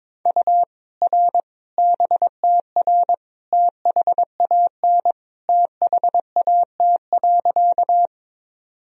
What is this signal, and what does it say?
Morse code, heard as a keyed tone, about 22 wpm, 700 Hz. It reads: U R BTR THAN THAT.
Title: Sentences from Top 400 Words and Abbreviations 22wpm